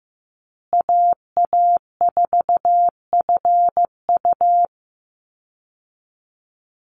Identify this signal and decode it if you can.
Morse code, heard as a keyed tone, about 15 wpm, 700 Hz. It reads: AA4FU
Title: Call Signs Repeated 2x - 15wpm